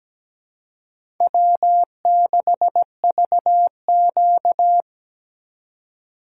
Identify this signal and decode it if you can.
Morse code, heard as a keyed tone, about 17 wpm, 700 Hz. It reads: W6VQ
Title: USA Call Signs - 4 Characters - 3x Character Spacing - 17wpm